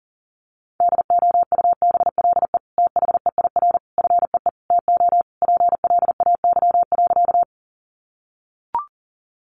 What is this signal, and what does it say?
Morse code, heard as a keyed tone, about 40 wpm, 700 Hz. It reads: DOUBLE THEIR FEE TO PLAY.
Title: Sentences from Top 1500 Words 40wpm